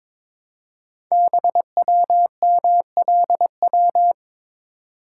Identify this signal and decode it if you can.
Morse code, heard as a keyed tone, about 22 wpm, 700 Hz. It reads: BWMLW
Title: Code Groups - 5 Letters - 22wpm